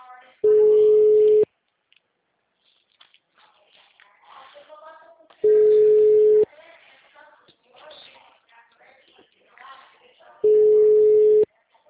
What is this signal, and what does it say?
Ringback tone (Europe, 425 Hz), with speech.